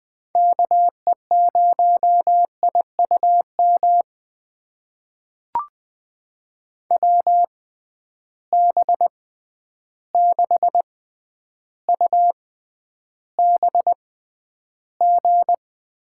Morse code, heard as a keyed tone, about 20 wpm, spaced as for 10 wpm, 700 Hz. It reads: KE0IUM WB6UBG
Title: USA Call Signs - 6 Characters - 6x Character Spacing - 20wpm